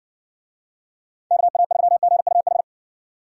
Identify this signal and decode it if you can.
Morse code, heard as a keyed tone, about 55 wpm, 700 Hz. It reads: KA3GUH